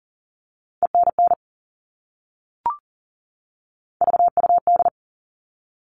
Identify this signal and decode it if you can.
Morse code, heard as a keyed tone, about 40 wpm, 700 Hz. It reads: ENN VUD
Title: All Cut Numbers - 3-Digit 40wpm